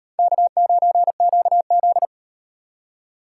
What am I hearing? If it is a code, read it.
Morse code, heard as a keyed tone, about 38 wpm, 700 Hz. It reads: K9QZ